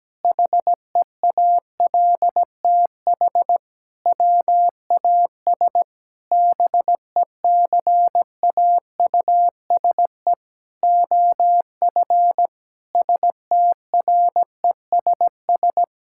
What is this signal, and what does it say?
Morse code, heard as a keyed tone, about 17 wpm, 700 Hz. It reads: HEALTH WAS BECAUSE OF STRESS
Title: Sentences from Top 1000 Words - Review Part 6 - 17wpm